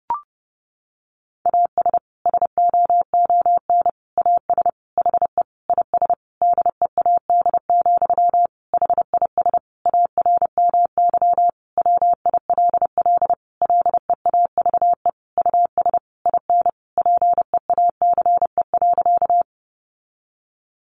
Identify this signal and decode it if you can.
Morse code, heard as a keyed tone, about 30 wpm, 700 Hz. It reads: AS SOON AS HE IS DEAD, HIS ARMY WILL LEAVE US IN PEACE.
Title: Sentences from Top 1500 Words 30wpm